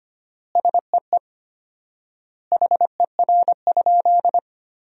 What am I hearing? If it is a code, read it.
Morse code, heard as a keyed tone, about 25 wpm, 700 Hz. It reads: SEE HER?
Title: 4x Word-Spacing - Sentences from Top 100 Words - 25wpm